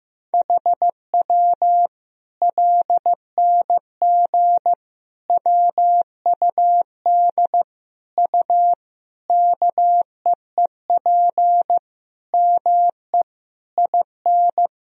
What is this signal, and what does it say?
Morse code, heard as a keyed tone, about 15 wpm, 700 Hz. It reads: HW LNG WUD U KEEP ME IN
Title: Sentences from Top 500 Words and Abbreviations 15wpm